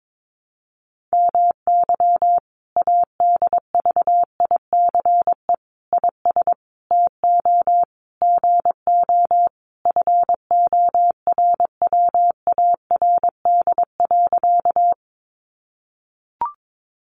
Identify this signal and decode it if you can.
Morse code, heard as a keyed tone, about 22 wpm, 700 Hz. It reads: MY ADVICE IS TO GO FORWARD.
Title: Sentences from Top 1300 Words 22wpm